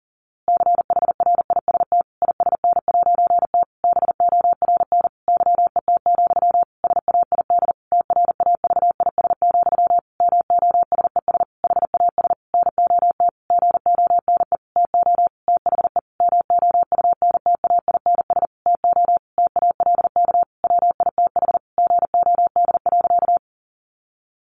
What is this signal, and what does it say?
Morse code, heard as a keyed tone, about 40 wpm, 700 Hz. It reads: CHRIST ISN'T BORN YET, SAID TRAVIS, MOSES HAS NOT GONE TO THE MOUNTAINS TO TALK WITH GOD.
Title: A Sound Of Thunder 40wpm